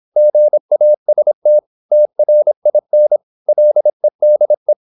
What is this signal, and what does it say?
Morse code, heard as a keyed tone, about 26 wpm, 600 Hz. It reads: GAST TRIN LEDE
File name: Korte ord DK 26wpm